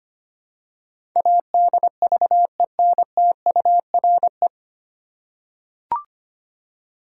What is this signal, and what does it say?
Morse code, heard as a keyed tone, about 25 wpm, 700 Hz. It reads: ADVENTURE
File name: practice / 25wpm / 9 Letter Words - First 3 Letters Given - 25wpm